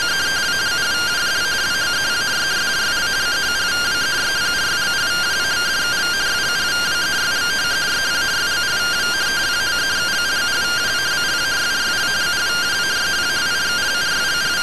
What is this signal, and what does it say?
Two-tone teleprinter signal, about 170 Hz shift.